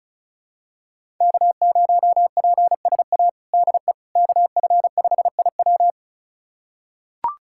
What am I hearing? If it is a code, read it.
Morse code, heard as a keyed tone, about 34 wpm, 700 Hz. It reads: K0PSA DE KF5IW
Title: USA Callsign DE Callsign - 5 Characters - 35wpm